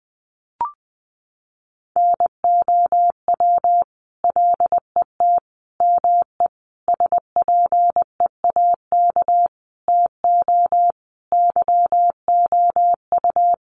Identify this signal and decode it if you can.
Morse code, heard as a keyed tone, about 20 wpm, 700 Hz. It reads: NOW LET ME SPEAK TO YOU
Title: Sentences from Top 1200 Words 20wpm